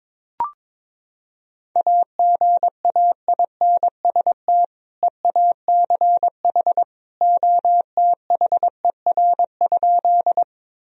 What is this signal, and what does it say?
Morse code, heard as a keyed tone, about 22 wpm, 700 Hz. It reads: AGAINST EACH OTHER?
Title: Sentences from Top 1000 Words - Review Part 1 - 22wpm